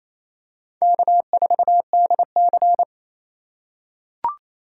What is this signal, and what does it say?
Morse code, heard as a keyed tone, about 28 wpm, 700 Hz. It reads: K4DC
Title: USA Call Signs - 4 Characters - 4x Character Spacing - 28wpm